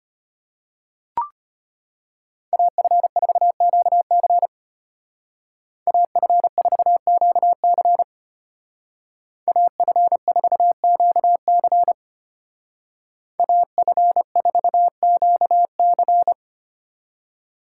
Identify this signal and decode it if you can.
Morse code, heard as a keyed tone, about 33 wpm, 700 Hz. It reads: AF4QC AF4QC AF4QC AF4QC